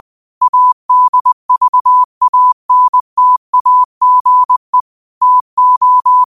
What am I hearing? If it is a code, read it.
Morse code, heard as a keyed tone, about 20 wpm, 1000 Hz. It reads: ADVANTAGE TO
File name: qotdfiles / 20wpm